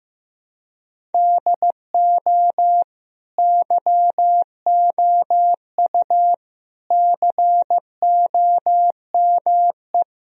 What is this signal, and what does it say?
Morse code, heard as a keyed tone, about 15 wpm, 700 Hz. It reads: DO YOU COME
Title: Sentences from Top 1000 Words - Review Part 1 - 15wpm